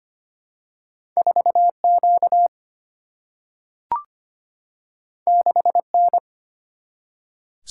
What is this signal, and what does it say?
Morse code, heard as a keyed tone, about 25 wpm, 700 Hz. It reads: 4Q 6N
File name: Two Letter-Number Combinations - 25wpm